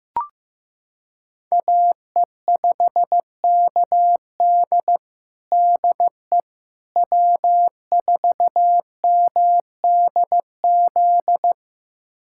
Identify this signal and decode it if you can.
Morse code, heard as a keyed tone, about 15 wpm, 700 Hz. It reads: AE5KD DE W4MDZ
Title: USA Callsign DE Callsign - 5 Characters - 15wpm